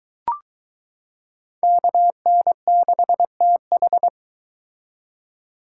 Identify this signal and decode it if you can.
Morse code, heard as a keyed tone, about 22 wpm, 700 Hz. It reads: KN6TH